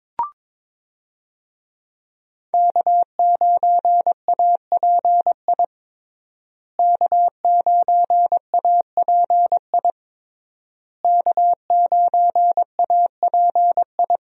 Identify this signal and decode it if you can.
Morse code, heard as a keyed tone, about 22 wpm, 700 Hz. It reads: K9API K9API K9API
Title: USA Call Signs - 5 Characters - Repeated 3x - 22wpm